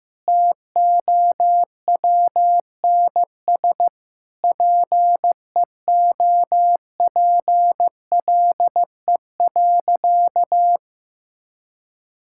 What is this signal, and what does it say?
Morse code, heard as a keyed tone, about 16 wpm, 700 Hz. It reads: TOWNS PEOPLE.
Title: Sentences from Top 1000 Words - Review Part 6 - 15wpm